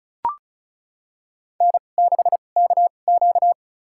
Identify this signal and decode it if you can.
Morse code, heard as a keyed tone, about 36 wpm, spaced as for 25 wpm, 700 Hz. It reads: N6KQ
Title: USA Call Signs - 4 Characters - 2x Character Spacing - 35wpm